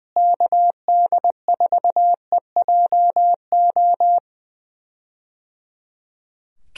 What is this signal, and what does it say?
Morse code, heard as a keyed tone, about 20 wpm, 700 Hz. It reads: KD4EJO
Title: USA Call Signs - 6 Characters - Repeated 3x - 20wpm